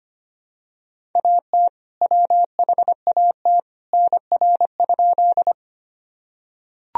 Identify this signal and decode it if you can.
Morse code, heard as a keyed tone, about 25 wpm, 700 Hz. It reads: AT WHAT NR?
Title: Sentences from Top 100 Words and Abbreviations 25wpm